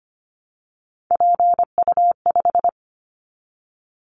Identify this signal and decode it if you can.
Morse code, heard as a keyed tone, about 25 wpm, 700 Hz. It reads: PU5
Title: Callsign Prefixes for Top 100 DXCC - 25wpm